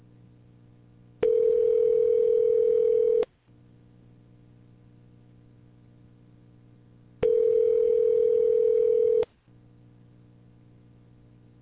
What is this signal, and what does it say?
Ringback tone (North America), 440 + 480 Hz.